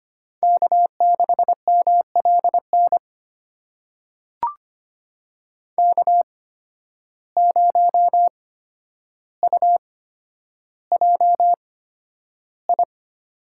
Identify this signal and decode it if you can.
Morse code, heard as a keyed tone, about 25 wpm, spaced as for 11 wpm, 700 Hz. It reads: K6MLN K0UJI
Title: USA Call Signs - 8x Character Spacing - 25wpm